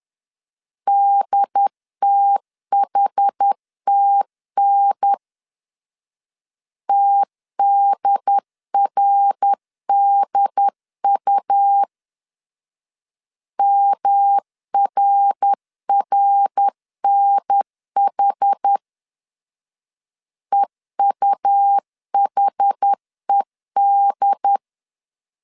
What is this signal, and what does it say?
Morse code, heard as a keyed tone, about 10 wpm, 800 Hz. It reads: DTHTN TDRDU MRRNH EUHED
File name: CW / DET / 1 3 1500C S C 10wpm 3335